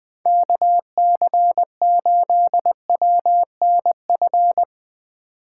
Morse code, heard as a keyed tone, about 20 wpm, 700 Hz. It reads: KC8WNF
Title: Speed Racing - USA 2x3 Call Signs 17wpm